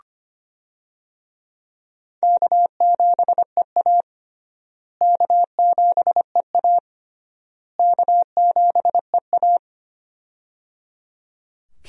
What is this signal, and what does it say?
Morse code, heard as a keyed tone, about 25 wpm, 700 Hz. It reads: K7EA K7EA K7EA
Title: USA Call Signs - 4 Characters - Repeated 3x - 25wpm